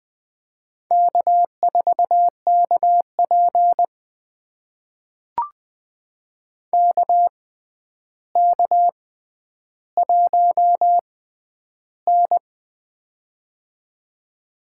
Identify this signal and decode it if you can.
Morse code, heard as a keyed tone, about 20 wpm, spaced as for 10 wpm, 700 Hz. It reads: K4KP KK1N
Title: USA Call Signs - 4 Characters - 6x Character Spacing - 20wpm